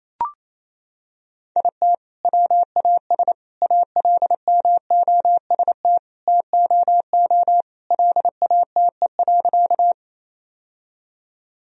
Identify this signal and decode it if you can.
Morse code, heard as a keyed tone, about 28 wpm, 700 Hz. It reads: IT WAS ALMOST TOO LATE.
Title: Sentences from Top 1000 Words - Review Part 4 - 28wpm